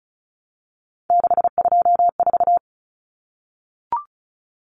Morse code, heard as a keyed tone, about 34 wpm, 700 Hz. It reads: 624
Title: Numbers - 3-Digit 35wpm